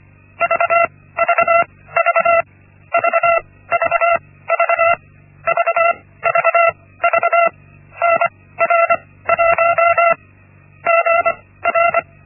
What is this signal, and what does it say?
Morse code, heard as a keyed tone, about 24 wpm, spaced as for 16 wpm, 680 Hz. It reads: VVV VVV VVV NR1 GR
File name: Linksmakalnis-1